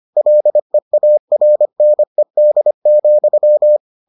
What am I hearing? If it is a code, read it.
Morse code, heard as a keyed tone, about 25 wpm, 600 Hz. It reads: LEARNED,